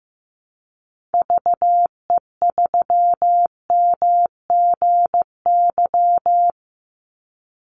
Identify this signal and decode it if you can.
Morse code, heard as a keyed tone, about 15 wpm, 700 Hz. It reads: VE3MGY
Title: Call Signs Repeated 3x - 15wpm